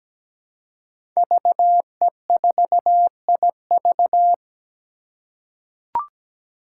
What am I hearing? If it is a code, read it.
Morse code, heard as a keyed tone, about 17 wpm, 700 Hz. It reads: VE4IV
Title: Five Challenging Letter-Number Combinations 17wpm